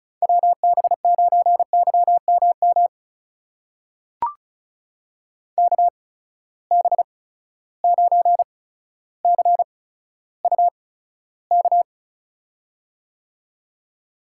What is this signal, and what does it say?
Morse code, heard as a keyed tone, about 36 wpm, spaced as for 15 wpm, 700 Hz. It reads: WB9YMM KB9CUK
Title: USA Call Signs - 6 Characters - 8x Character Spacing - 35wpm